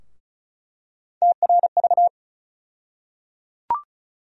Morse code, heard as a keyed tone, about 34 wpm, 700 Hz. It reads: TRV
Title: USA Call Sign Postfixes - 3 Letters 35wpm